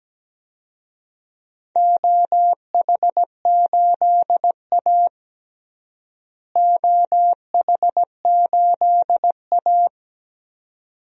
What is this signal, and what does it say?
Morse code, heard as a keyed tone, about 17 wpm, 700 Hz. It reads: OH8A OH8A
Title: DXCC-100 Call Signs - 4 Characters - Repeated 2x - 17wpm